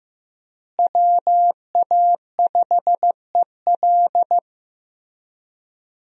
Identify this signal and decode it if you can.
Morse code, heard as a keyed tone, about 15 wpm, 700 Hz. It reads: WA5EL
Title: USA Call Signs - 5 Characters - Repeated 3x - 15wpm